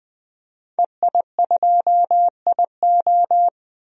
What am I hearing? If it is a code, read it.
Morse code, heard as a keyed tone, about 20 wpm, 700 Hz. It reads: EI2IO
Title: DXCC-100 Call Signs - 5 Characters - 20wpm